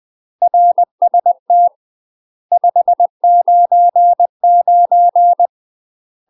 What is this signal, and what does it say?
Morse code, heard as a keyed tone, about 20 wpm, 700 Hz. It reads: RST 599
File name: UK MASTER QSOs TRAINING / 20WPM / UK MASTER QSOs 0028